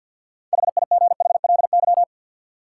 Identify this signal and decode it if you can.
Morse code, heard as a keyed tone, about 52 wpm, 700 Hz. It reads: HIGHLY
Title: Sentences from Top 1100 Words - New Word Drill - 50wpm